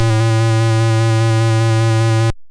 There is no code.